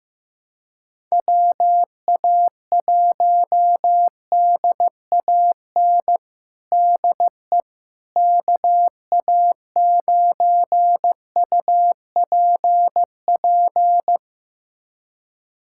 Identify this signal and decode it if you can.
Morse code, heard as a keyed tone, about 16 wpm, 700 Hz. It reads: WA1DAN DE KA9UPP
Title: USA Callsign DE Callsign 15wpm